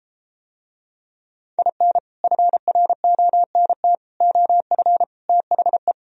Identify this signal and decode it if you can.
Morse code, heard as a keyed tone, about 32 wpm, 700 Hz. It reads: IN FRONT OF THE